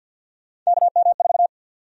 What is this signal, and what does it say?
Morse code, heard as a keyed tone, about 53 wpm, 700 Hz. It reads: KM4
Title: USA Call Sign Prefixes - 50wpm